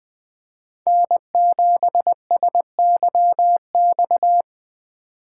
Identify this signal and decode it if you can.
Morse code, heard as a keyed tone, about 22 wpm, 700 Hz. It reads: N7SYX